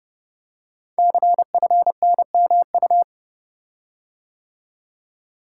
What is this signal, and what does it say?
Morse code, heard as a keyed tone, about 30 wpm, 700 Hz. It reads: CFNMU
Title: Code Groups - 5 Letters - 30wpm